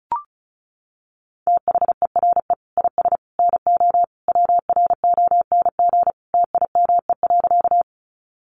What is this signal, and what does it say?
Morse code, heard as a keyed tone, about 34 wpm, 700 Hz. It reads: THERE IS NO WRONG TIME.
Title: Sentences from Top 900 Words 35wpm